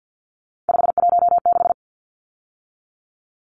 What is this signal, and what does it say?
Morse code, heard as a keyed tone, about 48 wpm, 700 Hz. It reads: 516
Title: Numbers - 3-Digit 50wpm